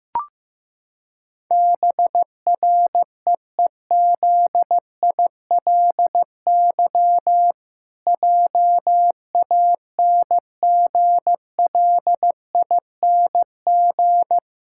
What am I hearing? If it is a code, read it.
Morse code, heard as a keyed tone, about 15 wpm, 700 Hz. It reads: BREEZILY JANGLING
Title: Pangram Challenge v2 - 15wpm